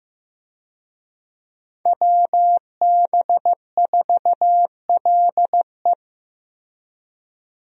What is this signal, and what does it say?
Morse code, heard as a keyed tone, about 15 wpm, 700 Hz. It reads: WB4LE
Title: USA Call Signs - 5 Characters - Repeated 2x - 15wpm